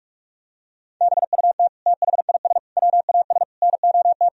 Morse code, heard as a keyed tone, about 46 wpm, 700 Hz. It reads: BUT THIS WAS NOT
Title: Sentences from Top 1000 Words - Review Part 3 - 45wpm